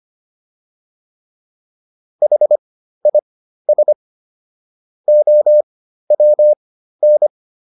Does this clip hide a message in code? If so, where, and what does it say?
Yes: Morse code from 2.2 to 7.3 s, heard as a keyed tone, about 25 wpm, spaced as for 13 wpm, 600 Hz. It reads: HIS OWN